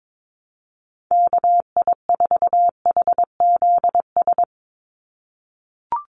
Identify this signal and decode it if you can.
Morse code, heard as a keyed tone, about 22 wpm, 700 Hz. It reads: KI4HZS
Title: USA Call Signs 22wpm